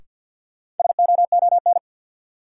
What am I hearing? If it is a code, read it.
Morse code, heard as a keyed tone, about 52 wpm, 700 Hz. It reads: SOON
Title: Sets of 3 Words - New Word Drill - 50wpm